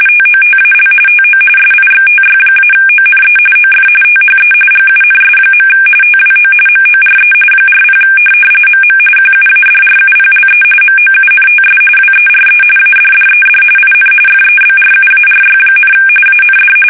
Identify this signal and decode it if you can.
RTTY, about 45.45 baud. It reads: TOTAL TO SIXTY-ONE OF THE GUNS. TWO MORE 40 MM QUADRUPLE MOUNTS WERE ADDED IN FEBRUARY, AND
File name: fsk850 / USS-Massachusetts